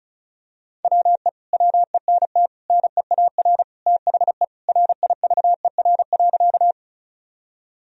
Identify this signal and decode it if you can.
Morse code, heard as a keyed tone, about 34 wpm, 700 Hz. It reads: WE WENT NEAR THE RIVER.
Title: Sentences from Top 1000 Words - Review Part 1 - 35wpm